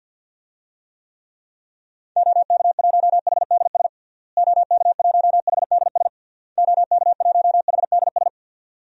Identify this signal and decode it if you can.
Morse code, heard as a keyed tone, about 52 wpm, 700 Hz. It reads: OK1HDS OK1HDS OK1HDS
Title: DXCC-100 Call Signs - 6 Characters - Repeated 3x - 50wpm